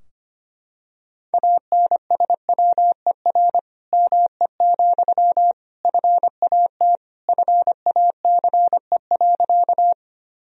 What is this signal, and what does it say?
Morse code, heard as a keyed tone, about 25 wpm, 700 Hz. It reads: ANSWER ME, FAT FACE.
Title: Sentences from Top 1000 Words - Review Part 4 - 25wpm